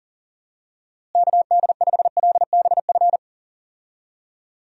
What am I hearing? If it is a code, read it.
Morse code, heard as a keyed tone, about 39 wpm, 700 Hz. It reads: KD5LBF
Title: USA Call Signs - 6 Characters - Repeated 3x - 40wpm